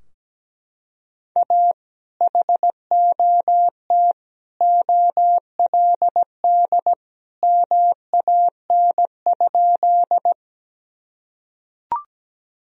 Morse code, heard as a keyed tone, about 17 wpm, 700 Hz. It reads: A HOT OLD MAN?
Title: Sentences from Top 1000 Words - Review Part 3 - 17wpm